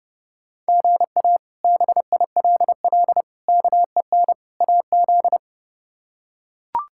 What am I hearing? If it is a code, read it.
Morse code, heard as a keyed tone, about 30 wpm, 700 Hz. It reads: GA BILL KEN AZ
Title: K1USN SST QSO Practice - 30wpm